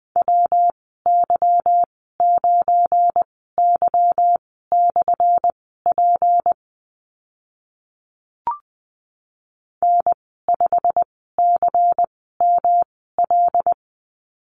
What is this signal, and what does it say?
Morse code, heard as a keyed tone, about 20 wpm, spaced as for 14 wpm, 700 Hz. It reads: WY9Y/P N5CML
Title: 179 Lesson 29g - Slash - Call Signs 20wpm